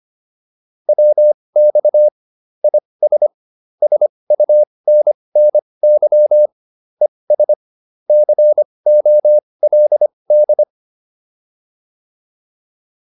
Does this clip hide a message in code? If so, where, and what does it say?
Yes: Morse code from 0.9 to 10.6 s, heard as a keyed tone, about 25 wpm, spaced as for 20 wpm, 600 Hz. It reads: WX IS SUNNY ES COLD